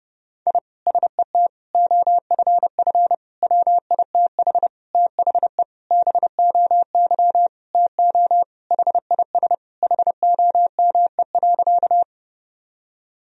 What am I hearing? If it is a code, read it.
Morse code, heard as a keyed tone, about 30 wpm, 700 Hz. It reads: I SET OFF WITH THE BOY TO HIS HOME.
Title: Sentences from Top 1000 Words - Review Part 1 - 30wpm